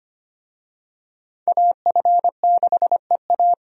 Morse code, heard as a keyed tone, about 25 wpm, 700 Hz. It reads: AF6EA
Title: USA Call Signs - 5 Characters - 25wpm